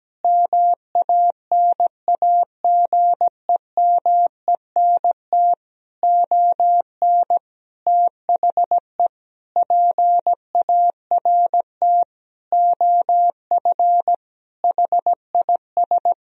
Morse code, heard as a keyed tone, about 17 wpm, 700 Hz. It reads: MANAGEMENT ON THE PART OF HIS
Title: Sentences from Top 1000 Words - Review Part 4 - 17wpm